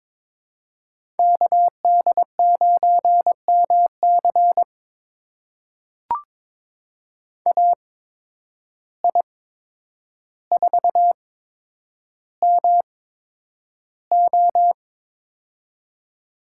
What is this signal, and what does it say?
Morse code, heard as a keyed tone, about 22 wpm, spaced as for 9 wpm, 700 Hz. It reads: KD9MC AI4MO
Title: USA Call Signs - 5 Characters - 8x Character Spacing - 22wpm